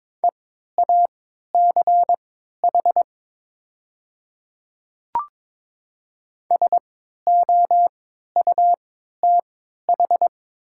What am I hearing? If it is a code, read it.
Morse code, heard as a keyed tone, about 22 wpm, spaced as for 12 wpm, 700 Hz. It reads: EACH SOUTH
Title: 3x Character Spacing - Top 500 Words - 22wpm